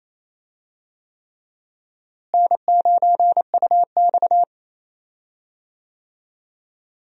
Morse code, heard as a keyed tone, about 28 wpm, 700 Hz. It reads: N9UX
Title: USA Call Signs - 4 Characters - Repeated 3x - 28wpm